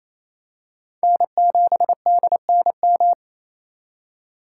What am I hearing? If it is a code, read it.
Morse code, heard as a keyed tone, about 28 wpm, 700 Hz. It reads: N7DNM
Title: USA Call Signs - 5 Characters - 4x Character Spacing - 28wpm